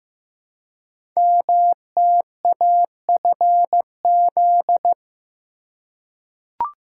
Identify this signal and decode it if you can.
Morse code, heard as a keyed tone, about 15 wpm, 700 Hz. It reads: MTAFZ
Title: Code Groups - 5 Letters - 15wpm